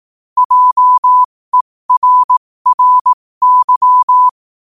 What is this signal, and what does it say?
Morse code, heard as a keyed tone, about 18 wpm, spaced as for 15 wpm, 1000 Hz. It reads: JERRY
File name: qotdfiles / 15wpm